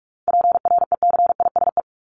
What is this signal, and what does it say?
Morse code, heard as a keyed tone, about 44 wpm, 700 Hz. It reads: PRECISE
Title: LY Suffix - Top 100 Words with Base Words 45wpm